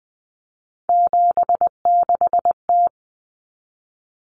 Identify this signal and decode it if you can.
Morse code, heard as a keyed tone, about 20 wpm, 700 Hz. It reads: 76T
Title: Common Cut Numbers - 3-Digit 20wpm